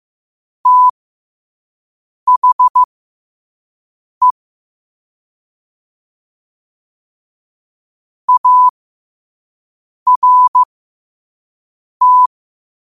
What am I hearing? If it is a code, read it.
Morse code, heard as a keyed tone, about 15 wpm, spaced as for 5 wpm, 1000 Hz. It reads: THE ART